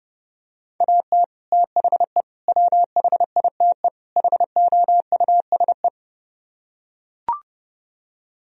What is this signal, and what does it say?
Morse code, heard as a keyed tone, about 30 wpm, 700 Hz. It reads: AT THE WHITE HOUSE